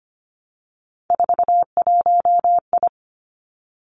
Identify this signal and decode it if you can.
Morse code, heard as a keyed tone, about 25 wpm, 700 Hz. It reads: 41I
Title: Three Challenging Letter-Number Combinations 25wpm